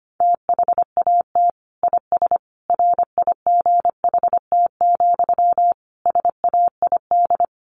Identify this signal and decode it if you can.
Morse code, heard as a keyed tone, about 25 wpm, 700 Hz. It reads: THAT IS RIGHT, SAID
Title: Sentences from Top 200 Words 25wpm